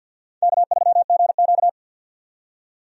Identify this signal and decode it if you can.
Morse code, heard as a keyed tone, about 52 wpm, 700 Hz. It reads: K3GQ
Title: USA Call Signs - 4 Characters - 50wpm